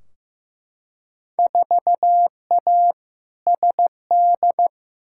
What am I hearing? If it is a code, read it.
Morse code, heard as a keyed tone, about 15 wpm, 700 Hz. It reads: 4A SD
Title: Field Day Exchanges - 15wpm